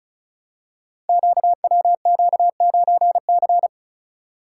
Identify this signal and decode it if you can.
Morse code, heard as a keyed tone, about 33 wpm, 700 Hz. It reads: QWQ9C
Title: Code Groups - 5 Letter:Numbers - 35wpm